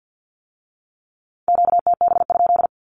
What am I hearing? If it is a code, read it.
Morse code, heard as a keyed tone, about 50 wpm, 700 Hz. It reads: QTB?
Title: Q-Codes with Spoken Meaning - 50wpm